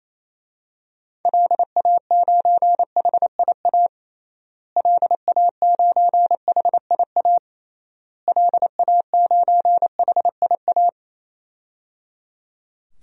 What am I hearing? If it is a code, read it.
Morse code, heard as a keyed tone, about 28 wpm, 700 Hz. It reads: LA9HIA LA9HIA LA9HIA
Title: DXCC-100 Call Signs - 6 Characters - Repeated 3x - 28wpm